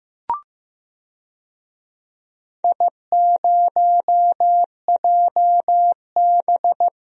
Morse code, heard as a keyed tone, about 15 wpm, 700 Hz. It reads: I0JB
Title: DXCC-100 Call Signs - 4 Characters - 15wpm